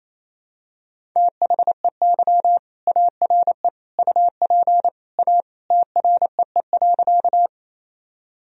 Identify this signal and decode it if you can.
Morse code, heard as a keyed tone, about 28 wpm, 700 Hz. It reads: THEY ARE UP A TREE.
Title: Sentences from Top 1000 Words - Review Part 1 - 28wpm